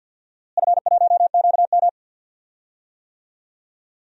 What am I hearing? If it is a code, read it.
Morse code, heard as a keyed tone, about 52 wpm, 700 Hz. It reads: F1QM
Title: DXCC-100 Call Signs - 4 Characters - Repeated 3x - 50wpm